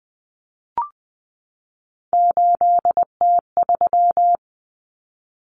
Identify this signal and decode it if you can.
Morse code, heard as a keyed tone, about 20 wpm, 700 Hz. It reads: 8T3
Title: Common Cut Numbers - 3-Digit 20wpm